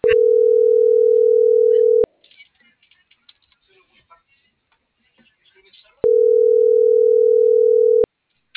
Ringback tone (North America), 440 + 480 Hz.